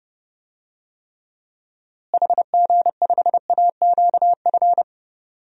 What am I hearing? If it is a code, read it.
Morse code, heard as a keyed tone, about 30 wpm, 700 Hz. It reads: HG5AQF
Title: DXCC-100 Call Signs - 6 Characters - Repeated 3x - 30wpm